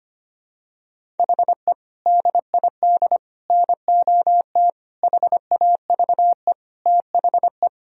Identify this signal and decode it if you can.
Morse code, heard as a keyed tone, about 25 wpm, 700 Hz. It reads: HE DID NOT HAVE THE
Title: Sentences from Top 1000 Words - Review Part 1 - 25wpm